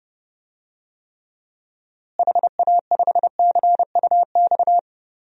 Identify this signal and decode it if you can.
Morse code, heard as a keyed tone, about 30 wpm, 700 Hz. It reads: HA5CUX
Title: DXCC-100 Call Signs - 6 Characters - Repeated 2x - 30wpm